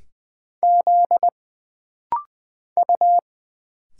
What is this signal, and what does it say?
Morse code, heard as a keyed tone, about 20 wpm, 700 Hz. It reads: Z U
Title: Single Letters - Rapid-Fire 20wpm